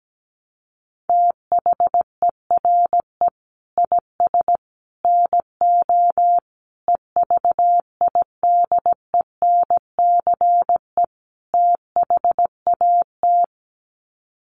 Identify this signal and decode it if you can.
Morse code, heard as a keyed tone, about 17 wpm, 700 Hz. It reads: THERE IS NO EVIDENCE THAT
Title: Sets of 5 Words Encore 17wpm